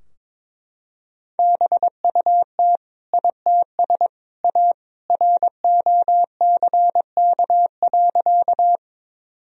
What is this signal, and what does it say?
Morse code, heard as a keyed tone, about 22 wpm, 700 Hz. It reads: BUT ITS A ROCK.
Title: Sentences from Top 1000 Words - Review Part 2 - 22wpm